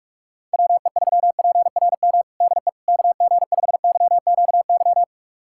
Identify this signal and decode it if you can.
Morse code, heard as a keyed tone, about 46 wpm, 700 Hz. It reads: WE3PRM DE KG5YQY
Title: USA Callsign DE Callsign - 6 Characters - 45wpm